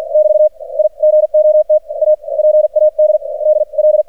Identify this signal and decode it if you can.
Morse code, heard as a keyed tone, about 48 wpm, spaced as for 36 wpm, 610 Hz. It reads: <BT> I GOT A PADDL
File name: archived / CRSnet